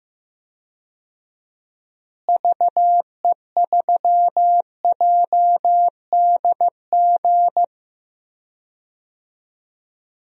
Morse code, heard as a keyed tone, about 16 wpm, 700 Hz. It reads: VE3JDG
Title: DXCC-100 Call Signs - 6 Characters - Repeated 3x - 15wpm